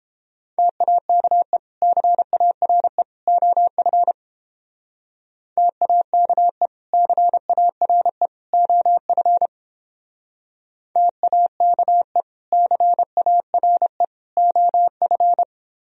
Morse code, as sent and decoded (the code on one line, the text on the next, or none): - .- -.- . / -.-. .- .-. / --- ..-. / - .- -.- . / -.-. .- .-. / --- ..-. / - .- -.- . / -.-. .- .-. / --- ..-.
TAKE CARE OF TAKE CARE OF TAKE CARE OF